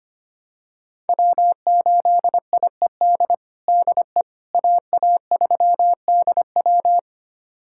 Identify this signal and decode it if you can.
Morse code, heard as a keyed tone, about 25 wpm, 700 Hz. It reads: W8IED DE AA3DW
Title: USA Callsign DE Callsign - 5 Characters - 25wpm